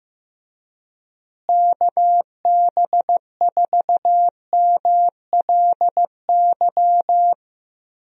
Morse code, heard as a keyed tone, about 15 wpm, 700 Hz. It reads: KB4MLY
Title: USA Call Signs - 6 Characters - Repeated 2x - 15wpm